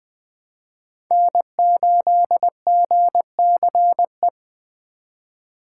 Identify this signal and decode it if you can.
Morse code, heard as a keyed tone, about 20 wpm, 700 Hz. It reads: N8GCE